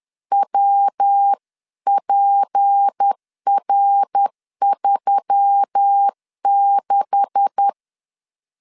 Morse code, heard as a keyed tone, about 10 wpm, 800 Hz. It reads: WPR36
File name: LEZIONI / 10 SIGNS / 10A SLASH 10wpm 2808